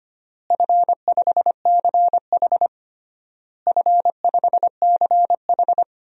Morse code, heard as a keyed tone, about 25 wpm, 700 Hz. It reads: F5CH F5CH
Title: DXCC-100 Call Signs - 4 Characters - Repeated 3x - 25wpm